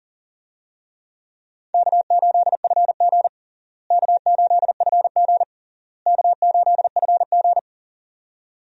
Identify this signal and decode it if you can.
Morse code, heard as a keyed tone, about 40 wpm, 700 Hz. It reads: K8FG K8FG K8FG
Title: USA Call Signs - 4 Characters - Repeated 3x - 40wpm